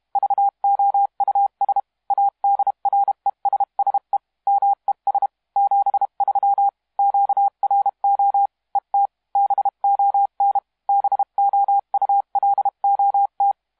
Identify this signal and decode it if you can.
Morse code, heard as a keyed tone, about 32 wpm, 800 Hz. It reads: VOUS ADRESSE MES 73 QRO ET BON BOULOT
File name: musiqueradio / cw hsc demo f6ddr